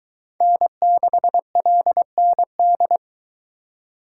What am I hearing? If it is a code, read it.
Morse code, heard as a keyed tone, about 21 wpm, 700 Hz. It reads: N6LND